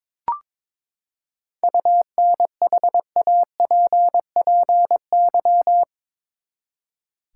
Morse code, heard as a keyed tone, about 22 wpm, 700 Hz. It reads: UNHAPPY
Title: Seven Letter Words - 22wpm